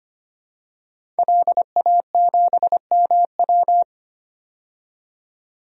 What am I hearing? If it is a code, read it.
Morse code, heard as a keyed tone, about 25 wpm, 700 Hz. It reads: LA7MW
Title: DXCC-100 Call Signs - 5 Characters - 25wpm